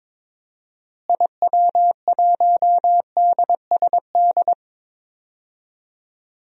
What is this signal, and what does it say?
Morse code, heard as a keyed tone, about 21 wpm, 700 Hz. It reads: IW1DSD
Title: DXCC-100 Call Signs - 6 Characters - Repeated 2x - 22wpm